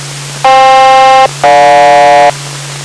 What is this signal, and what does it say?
Aircraft SELCAL, heard as two pairs of tones. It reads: FK-GJ